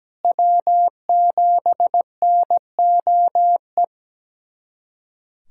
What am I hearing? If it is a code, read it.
Morse code, heard as a keyed tone, about 17 wpm, 700 Hz. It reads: W7NOE
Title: USA Callsign DE Callsign - 5 Characters - 17wpm